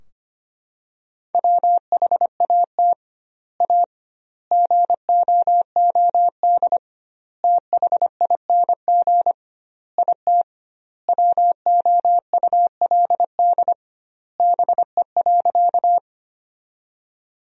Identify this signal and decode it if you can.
Morse code, heard as a keyed tone, about 25 wpm, 700 Hz. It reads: WHAT A GOOD THING IT WOULD BE.
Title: 2x Word-Spacing - Sentences from Top 200 Words - 25wpm